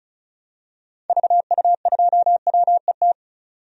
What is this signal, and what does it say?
Morse code, heard as a keyed tone, about 34 wpm, 700 Hz. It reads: VU2WET
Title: DXCC-100 Call Signs - 6 Characters - 35wpm